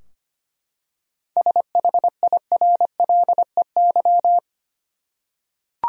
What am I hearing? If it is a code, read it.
Morse code, heard as a keyed tone, about 25 wpm, 700 Hz. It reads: SHIRLEY
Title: USA Common Names - 25wpm